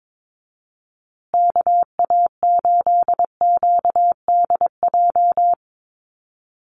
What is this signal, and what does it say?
Morse code, heard as a keyed tone, about 22 wpm, 700 Hz. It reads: KA8QDJ
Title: USA Call Signs Repeated 3x 22wpm